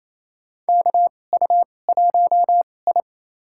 Morse code, heard as a keyed tone, about 28 wpm, spaced as for 20 wpm, 700 Hz. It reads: KU1I
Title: USA Call Signs - 2x Character Spacing - 28wpm